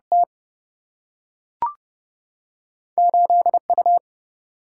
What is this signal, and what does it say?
Morse code, heard as a keyed tone, about 30 wpm, 700 Hz. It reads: T 8U
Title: Single-Letter Single-Number 30wpm